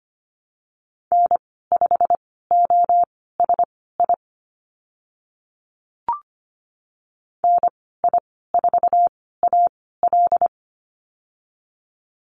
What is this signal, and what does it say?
Morse code, heard as a keyed tone, about 25 wpm, spaced as for 16 wpm, 690 Hz. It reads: N5OSI NI4AL
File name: Lesson 15g - Number 5 - Call Signs 25wpm